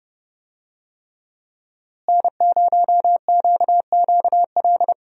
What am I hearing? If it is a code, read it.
Morse code, heard as a keyed tone, about 32 wpm, 700 Hz. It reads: N0QQL
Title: USA Call Signs - 5 Characters - Repeated 3x - 30wpm